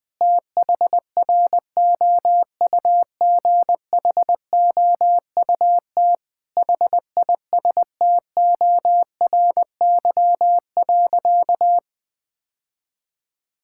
Morse code, heard as a keyed tone, about 20 wpm, 700 Hz. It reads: THROUGHOUT HISTORY.
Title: Sentences from Top 1000 Words - Review Part 5 - 20wpm